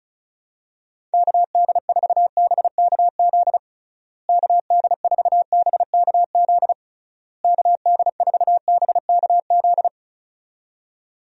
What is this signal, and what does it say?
Morse code, heard as a keyed tone, about 36 wpm, 700 Hz. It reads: KD4BKZ KD4BKZ KD4BKZ
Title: USA Call Signs - 6 Characters - Repeated 3x - 35wpm